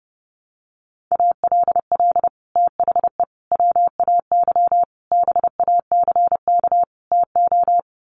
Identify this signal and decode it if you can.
Morse code, heard as a keyed tone, about 30 wpm, 700 Hz. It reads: ALL THE WAY BACK TO
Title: Sets of 5 Words Encore 30wpm